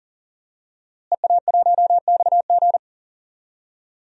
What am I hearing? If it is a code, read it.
Morse code, heard as a keyed tone, about 40 wpm, 690 Hz. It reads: EA1XG
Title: DXCC-100 Call Signs - 5 Characters - Repeated 2x - 40wpm